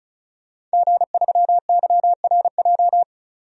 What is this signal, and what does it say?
Morse code, heard as a keyed tone, about 34 wpm, 700 Hz. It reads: G3YRJ
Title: DXCC-100 Call Signs - 5 Characters - 35wpm